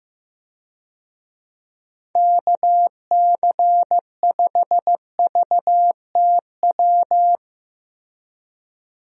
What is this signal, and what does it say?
Morse code, heard as a keyed tone, about 16 wpm, 700 Hz. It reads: KC5VTW
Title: USA Call Signs - 6 Characters - Repeated 2x - 15wpm